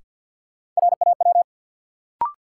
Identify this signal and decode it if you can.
Morse code, heard as a keyed tone, about 53 wpm, 700 Hz. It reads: RAW
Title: Sentences from Top 1500 Words - New Word Drill - 50wpm